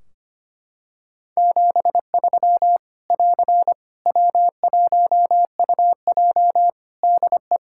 Morse code, heard as a keyed tone, about 25 wpm, 700 Hz. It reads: 73 <AR> W1UJ DE
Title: Practice QSO - Exchange 4 - Ending - 25wpm